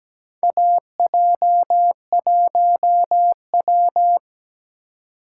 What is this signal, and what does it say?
Morse code, heard as a keyed tone, about 17 wpm, 700 Hz. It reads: AJ1W
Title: USA Call Signs - 4 Characters - 2x Character Spacing - 17wpm